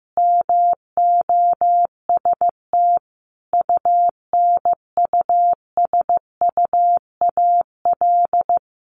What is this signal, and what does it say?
Morse code, heard as a keyed tone, about 15 wpm, 700 Hz. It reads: MOST UNUSUAL
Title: Sentences from Top 1400 Words 15wpm